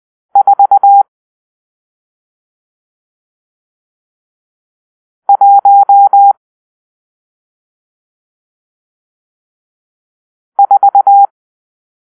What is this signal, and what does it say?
Morse code, heard as a keyed tone, about 20 wpm, 800 Hz. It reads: 4 1 4
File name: arrl / Grp-6 Drill-1